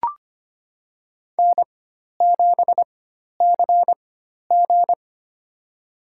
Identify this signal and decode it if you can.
Morse code, heard as a keyed tone, about 25 wpm, spaced as for 12 wpm, 700 Hz. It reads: N7CG
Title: USA Call Signs - 4 Characters - 4x Character Spacing - 25wpm